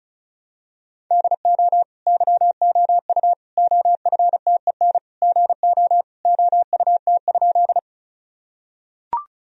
Morse code, heard as a keyed tone, about 34 wpm, 700 Hz. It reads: DO YOU OFTEN GO OUT?
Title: Sentences from Top 1000 Words - Review Part 1 - 35wpm